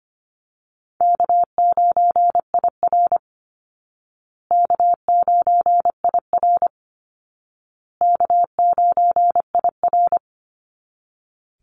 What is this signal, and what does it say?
Morse code, heard as a keyed tone, about 24 wpm, 700 Hz. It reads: K9IR K9IR K9IR
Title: Easy Call Signs Repeated 3x - 25wpm